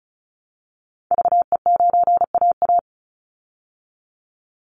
Morse code, heard as a keyed tone, about 34 wpm, 700 Hz. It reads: VE9AA
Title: Call Signs Repeated 3x - 35wpm